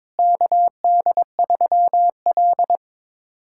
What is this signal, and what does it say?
Morse code, heard as a keyed tone, about 22 wpm, 700 Hz. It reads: KD3L
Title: USA Callsign DE Callsign - 4 Characters - 22wpm